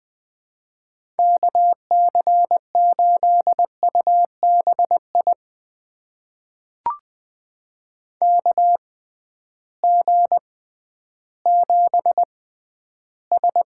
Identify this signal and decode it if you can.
Morse code, heard as a keyed tone, about 20 wpm, spaced as for 12 wpm, 700 Hz. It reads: KC8UBI KG7S
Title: USA Call Signs - 6 Characters - 6x Character Spacing - 20wpm